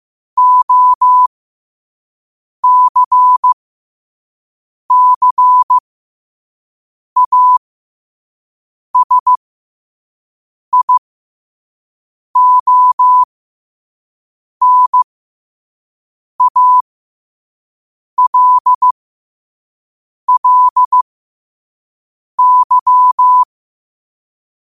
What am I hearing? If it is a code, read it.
Morse code, heard as a keyed tone, about 15 wpm, spaced as for 5 wpm, 1000 Hz. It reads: OCCASIONALLY
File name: qotdfiles / 5wpm